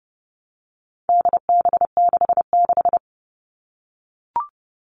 Morse code, heard as a keyed tone, about 30 wpm, 700 Hz. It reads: DB66
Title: All Cut Numbers - 4-Digit 30wpm